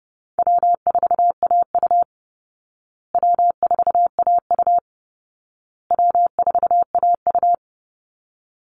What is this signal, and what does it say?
Morse code, heard as a keyed tone, about 30 wpm, 700 Hz. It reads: W4AU W4AU W4AU
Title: Easy Call Signs Repeated 3x - 30wpm